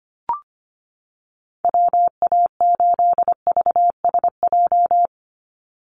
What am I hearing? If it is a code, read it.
Morse code, heard as a keyed tone, about 25 wpm, 700 Hz. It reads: WA8VSJ
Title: Call Signs - 25wpm